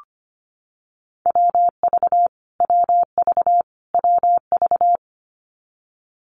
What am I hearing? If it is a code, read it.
Morse code, heard as a keyed tone, about 25 wpm, 700 Hz. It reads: WV WV WV
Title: US States Repeated 3x 25wpm